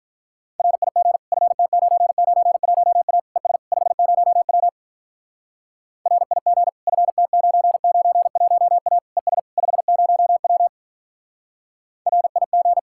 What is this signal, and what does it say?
Morse code, heard as a keyed tone, about 44 wpm, 700 Hz. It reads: RIG FT991A ES 50W RIG FT991A ES 50W RIG